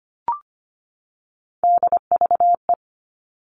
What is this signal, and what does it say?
Morse code, heard as a keyed tone, about 25 wpm, 700 Hz. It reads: DVE
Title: All Cut Numbers - 3-Digit 25wpm